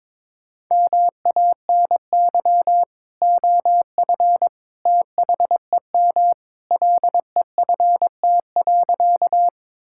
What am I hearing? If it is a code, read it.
Morse code, heard as a keyed tone, about 22 wpm, 700 Hz. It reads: MANY OF THEM LEFT.
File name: practice / 22wpm / Sentences with 2-Word Phrases - 22wpm